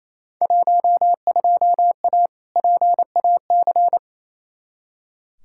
Morse code, heard as a keyed tone, about 28 wpm, 700 Hz. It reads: 12A PAC
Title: Field Day Exchanges Encore - 28wpm